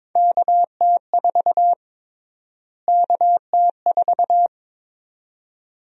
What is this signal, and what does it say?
Morse code, heard as a keyed tone, about 21 wpm, 700 Hz. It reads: KT4 KT4
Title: USA Call Sign Prefixes - Repeated 2x - 22wpm